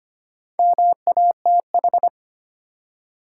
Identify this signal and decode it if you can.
Morse code, heard as a keyed tone, about 25 wpm, 700 Hz. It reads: MATH
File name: practice / 25wpm / Easy Words with Hints - 25wpm